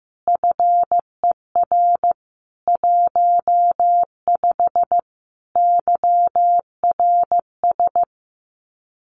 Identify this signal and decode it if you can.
Morse code, heard as a keyed tone, about 15 wpm, 700 Hz. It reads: FER 15 YRS
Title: Common QSO Phrases Encore 15wpm